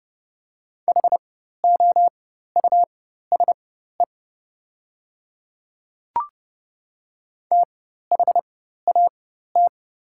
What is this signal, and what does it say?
Morse code, heard as a keyed tone, about 31 wpm, spaced as for 14 wpm, 700 Hz. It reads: HOUSE THAT
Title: 4x Character Spacing - Top 500 Words - 30wpm